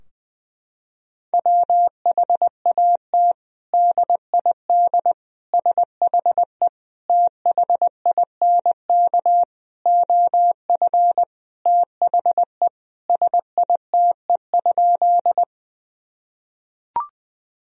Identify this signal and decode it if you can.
Morse code, heard as a keyed tone, about 20 wpm, 700 Hz. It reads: WHAT DID SHE THINK OF THE SITE?
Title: Sentences from Top 1000 Words - Review Part 4 - 20wpm